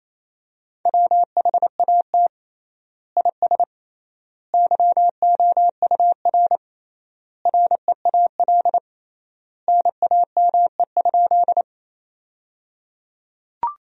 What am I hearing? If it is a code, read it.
Morse code, heard as a keyed tone, about 28 wpm, 700 Hz. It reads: WHAT IS YOUR REAL NAME?
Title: 3x Word-Spacing - Sentences from Top 300 Words - 28wpm